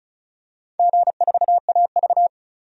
Morse code, heard as a keyed tone, about 36 wpm, 700 Hz. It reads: G4AV
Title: DXCC-100 Call Signs - 4 Characters - 35wpm